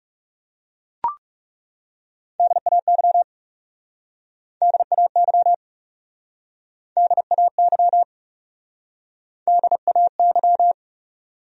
Morse code, heard as a keyed tone, about 40 wpm, 700 Hz. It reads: DAY DAY DAY DAY